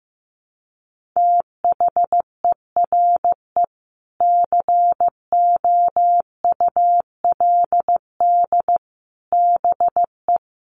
Morse code, heard as a keyed tone, about 15 wpm, 700 Hz. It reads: THERE COULD BE
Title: Sentences from Top 1200 Words 15wpm